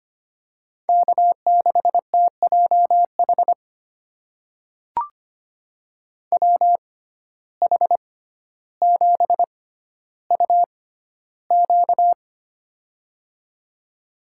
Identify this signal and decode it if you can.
Morse code, heard as a keyed tone, about 25 wpm, spaced as for 13 wpm, 700 Hz. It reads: K6TJH WH7UQ
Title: USA Call Signs - 6x Character Spacing - 25wpm